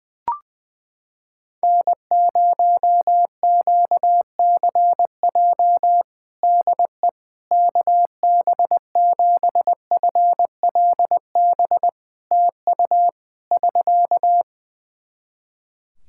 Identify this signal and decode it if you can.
Morse code, heard as a keyed tone, about 21 wpm, 700 Hz. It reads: N0QCJ DE KB7FLB TU <SK>
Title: Practice QSO - Exchange 4 - Ending - 20wpm